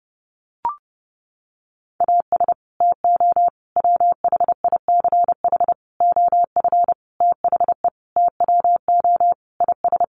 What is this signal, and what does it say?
Morse code, heard as a keyed tone, about 30 wpm, 700 Hz. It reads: AS TO WHICH OF THE TWO IS
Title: Sentences from Top 1200 Words 30wpm